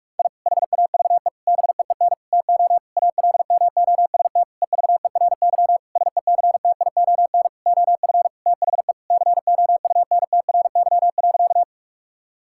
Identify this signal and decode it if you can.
Morse code, heard as a keyed tone, about 41 wpm, 700 Hz. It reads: I HAVE BEEN TO ALMOST EVERY SECTION OF THE COUNTRY.
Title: Sentences from Top 1000 Words - Review Part 5 - 45wpm